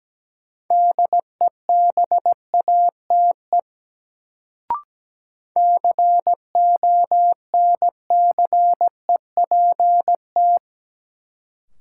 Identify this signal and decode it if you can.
Morse code, heard as a keyed tone, about 17 wpm, 700 Hz. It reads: DEBATE CONCEPT
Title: Sentences from Top 1100 Words - New Word Drill - 17wpm